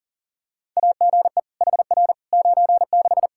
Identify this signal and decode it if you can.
Morse code, heard as a keyed tone, about 40 wpm, 700 Hz. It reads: AGE HR 96
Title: K1USN SST QSO Practice 40wpm